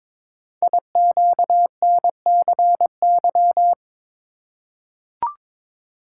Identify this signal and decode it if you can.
Morse code, heard as a keyed tone, about 22 wpm, 700 Hz. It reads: IQNCY
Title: Code Groups - 5 Letters - 22wpm